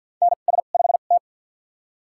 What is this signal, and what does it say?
Morse code, heard as a keyed tone, about 51 wpm, spaced as for 36 wpm, 700 Hz. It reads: NS5T
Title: USA Call Signs - 4 Characters - 2x Character Spacing - 50wpm